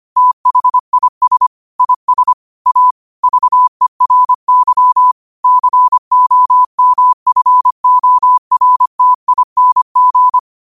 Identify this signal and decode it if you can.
Morse code, heard as a keyed tone, about 25 wpm, 1000 Hz. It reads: THIS IS A VERY COMFORTING